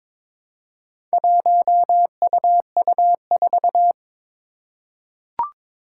Morse code, heard as a keyed tone, about 23 wpm, 700 Hz. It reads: 1UU4
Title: Four Challenging Letter-Number Combinations 22wpm